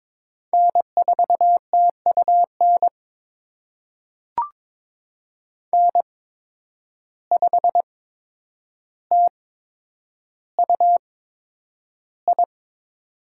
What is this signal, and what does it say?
Morse code, heard as a keyed tone, about 22 wpm, spaced as for 9 wpm, 700 Hz. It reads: N4TUN N5TUI
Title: USA Call Signs - 5 Characters - 8x Character Spacing - 22wpm